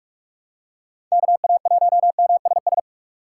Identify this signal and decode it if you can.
Morse code, heard as a keyed tone, about 46 wpm, 700 Hz. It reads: KA1MSS
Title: USA Call Signs - 6 Characters - 3x Character Spacing - 45wpm